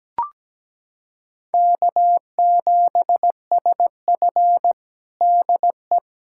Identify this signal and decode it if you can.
Morse code, heard as a keyed tone, about 17 wpm, 700 Hz. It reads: K7SF DE
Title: K1USN SST QSO Practice 17wpm